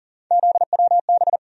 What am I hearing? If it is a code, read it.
Morse code, heard as a keyed tone, about 40 wpm, 700 Hz. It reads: ZWB
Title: USA Call Sign Postfixes - 3 Letters 40wpm